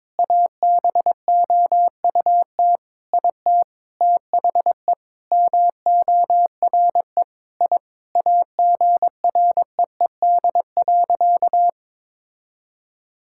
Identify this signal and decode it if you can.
Morse code, heard as a keyed tone, about 22 wpm, 700 Hz. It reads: ABOUT IT THE MORE I AGREED.
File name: Sentences with 9-Word Phrases - 22wpm